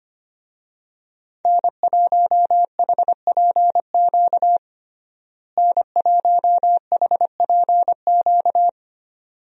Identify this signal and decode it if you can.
Morse code, heard as a keyed tone, about 25 wpm, 700 Hz. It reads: N1HPQ N1HPQ
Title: USA Call Signs - 5 Characters - Repeated 2x - 25wpm